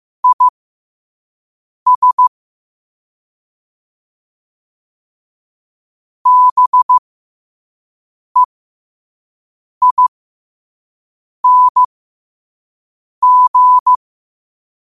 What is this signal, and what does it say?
Morse code, heard as a keyed tone, about 15 wpm, spaced as for 5 wpm, 1000 Hz. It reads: IS BEING
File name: qotdfiles / 5wpm